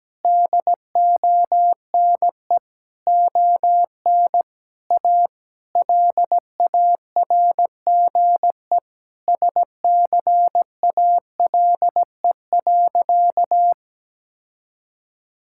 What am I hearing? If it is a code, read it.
Morse code, heard as a keyed tone, about 17 wpm, 700 Hz. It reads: DONE ON A LARGE SCALE.
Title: Sentences from Top 1000 Words - Review Part 5 - 17wpm